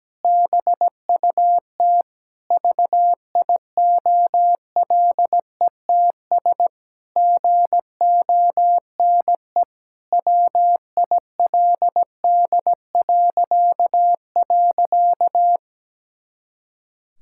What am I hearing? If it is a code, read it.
Morse code, heard as a keyed tone, about 17 wpm, 700 Hz. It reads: BUT VIOLETS GONE WILD..
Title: Pangram Challenge v2 - 17wpm